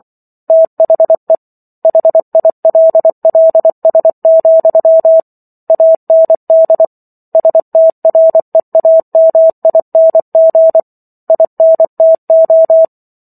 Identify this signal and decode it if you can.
Morse code, heard as a keyed tone, about 24 wpm, 650 Hz. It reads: THE HILLS, AND STREAMING INTO